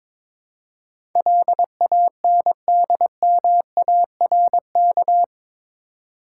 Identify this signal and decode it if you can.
Morse code, heard as a keyed tone, about 22 wpm, 700 Hz. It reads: LANDMARK
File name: practice / 22wpm / Eight Letter Words - 22wpm